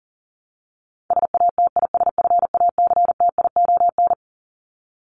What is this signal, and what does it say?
Morse code, heard as a keyed tone, about 40 wpm, 700 Hz. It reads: SATISFACTION
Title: Top 1751-2000 Words 40wpm